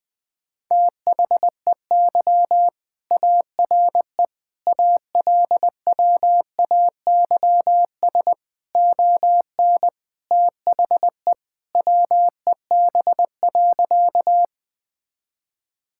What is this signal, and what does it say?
Morse code, heard as a keyed tone, about 20 wpm, 700 Hz. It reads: THEY ARE ALWAYS ON THE WEB.
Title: All-Age Appropriate Jokes - 20wpm